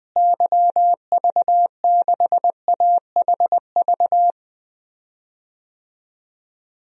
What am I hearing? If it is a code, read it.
Morse code, heard as a keyed tone, about 19 wpm, 700 Hz. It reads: YV6AHV
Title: DXCC-100 Call Signs - 6 Characters - Repeated 2x - 20wpm